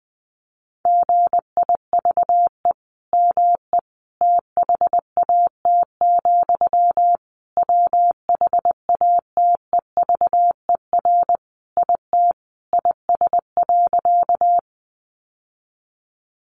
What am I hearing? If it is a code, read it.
Morse code, heard as a keyed tone, about 20 wpm, 700 Hz. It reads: GIVE ME THAT, WHATEVER IT IS.
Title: Sentences from Top 1100 Words 20wpm